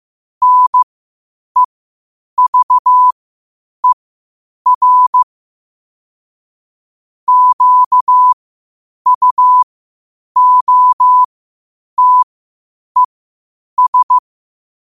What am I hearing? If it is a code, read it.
Morse code, heard as a keyed tone, about 15 wpm, spaced as for 8 wpm, 1000 Hz. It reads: NEVER QUOTES
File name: qotdfiles / 8wpm